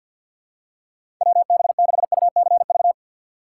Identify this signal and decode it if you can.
Morse code, heard as a keyed tone, about 50 wpm, 700 Hz. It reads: WB6UCV
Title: USA Call Signs - 6x Character Spacing - 50wpm